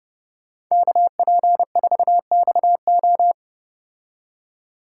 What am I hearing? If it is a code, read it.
Morse code, heard as a keyed tone, about 30 wpm, 700 Hz. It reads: KP4XO